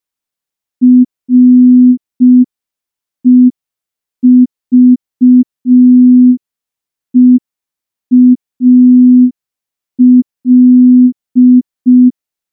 Morse code, heard as a keyed tone, about 5 wpm, 250 Hz. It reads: REVEAL
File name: morse / morse-training-05WPM